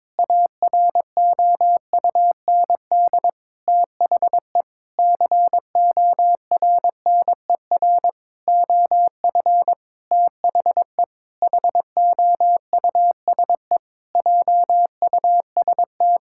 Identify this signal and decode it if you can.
Morse code, heard as a keyed tone, about 22 wpm, 700 Hz. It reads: AROUND THE CORNER OF THE HOUSE JUST
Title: Sentences from Top 1600 Words 22wpm